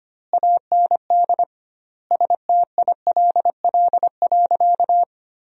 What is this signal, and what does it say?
Morse code, heard as a keyed tone, about 25 wpm, 700 Hz. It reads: AND STILL.
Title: 2x Word-Spacing - Sentences from Top 400 Words - 25wpm